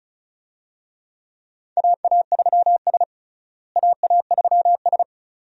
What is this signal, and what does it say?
Morse code, heard as a keyed tone, about 36 wpm, 700 Hz. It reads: AA3S AA3S
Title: USA Call Signs - 4 Characters - Repeated 2x - 35wpm